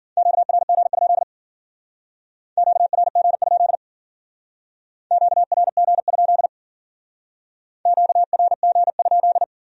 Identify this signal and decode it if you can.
Morse code, heard as a keyed tone, about 50 wpm, 700 Hz. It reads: QRG? QRG? QRG? QRG?
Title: Speed-Racing Q-Codes - 40wpm